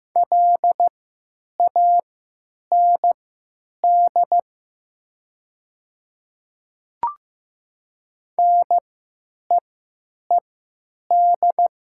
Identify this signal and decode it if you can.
Morse code, heard as a keyed tone, about 15 wpm, spaced as for 9 wpm, 700 Hz. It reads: LAND NEED
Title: 3x Character Spacing - Top 200 Words - 15wpm